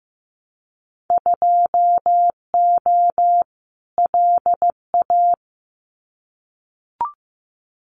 Morse code, heard as a keyed tone, about 15 wpm, 700 Hz. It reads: 2O LA
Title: Winter Field Day Exchanges 15wpm